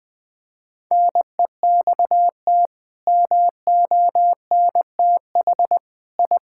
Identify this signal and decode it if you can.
Morse code, heard as a keyed tone, about 20 wpm, 700 Hz. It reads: NEXT MONTH I
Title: Sentences from Top 1000 Words - Review Part 5 - 20wpm